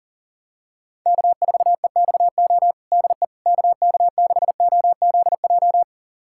Morse code, heard as a keyed tone, about 40 wpm, 700 Hz. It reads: K4EXO DE KK6OZJ
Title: USA Callsign DE Callsign 40wpm